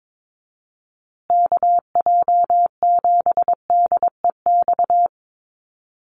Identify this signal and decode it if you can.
Morse code, heard as a keyed tone, about 22 wpm, 700 Hz. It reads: KJ7DEX
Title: USA Call Signs Repeated 3x 22wpm